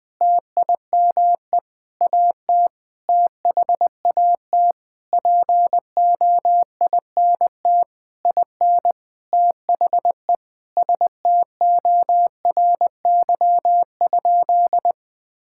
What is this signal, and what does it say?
Morse code, heard as a keyed tone, about 20 wpm, 700 Hz. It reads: TIME AT THAT POINT IN THE STORY?
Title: Sentences from Top 1000 Words - Review Part 5 - 20wpm